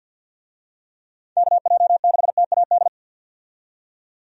Morse code, heard as a keyed tone, about 50 wpm, 700 Hz. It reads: KJ6TAD
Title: USA Call Signs - 6 Characters - 50wpm